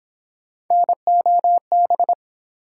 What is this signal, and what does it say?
Morse code, heard as a keyed tone, about 31 wpm, 700 Hz. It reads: NOB